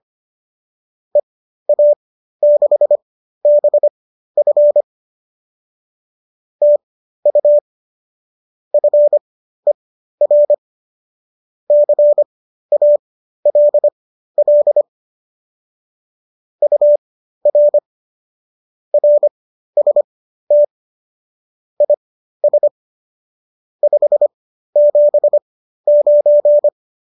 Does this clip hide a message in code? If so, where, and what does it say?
Yes: Morse code from 1.1 to 26.7 s, heard as a keyed tone, about 25 wpm, spaced as for 13 wpm, 590 Hz. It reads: EA6BF TU FER CALL UR RST IS 579